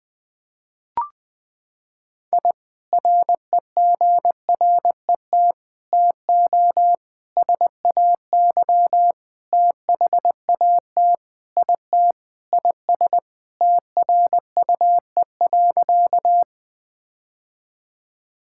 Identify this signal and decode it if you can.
Morse code, heard as a keyed tone, about 20 wpm, 700 Hz. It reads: I REGRET TO SAY THAT IT IS TRUE.
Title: Sentences with 3-Word Phrases - 20wpm